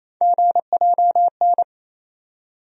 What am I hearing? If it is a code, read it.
Morse code, heard as a keyed tone, about 28 wpm, 700 Hz. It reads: GJN
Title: USA Call Sign Postfixes - 3 Letters 28wpm